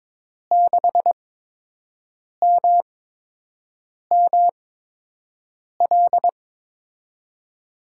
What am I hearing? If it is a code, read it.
Morse code, heard as a keyed tone, about 22 wpm, spaced as for 6 wpm, 700 Hz. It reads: 6MML
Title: USA Call Signs - 5 Characters - 8x Character Spacing - 22wpm